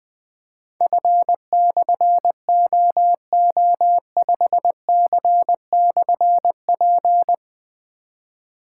Morse code, heard as a keyed tone, about 20 wpm, 700 Hz. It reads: F/OO5C/P
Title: Difficult Call Signs 20wpm